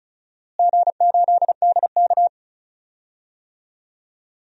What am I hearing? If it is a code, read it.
Morse code, heard as a keyed tone, about 36 wpm, 700 Hz. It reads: G8DK
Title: DXCC-100 Call Signs - 4 Characters - Repeated 3x - 35wpm